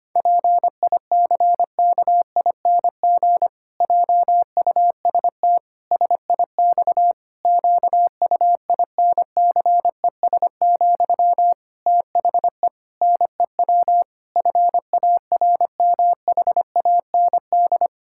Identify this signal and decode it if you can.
Morse code, heard as a keyed tone, about 25 wpm, 700 Hz. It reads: PICKING JUST SIX QUINCES, THE NEW FARMHAND
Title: Pangram Challenge v2 - 25wpm